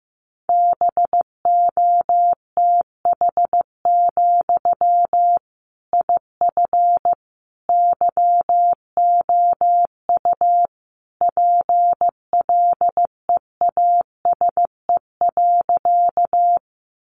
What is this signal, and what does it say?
Morse code, heard as a keyed tone, about 15 wpm, 700 Hz. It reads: BOTH, IF YOU PLEASE.
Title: Sentences from Top 1500 Words 15wpm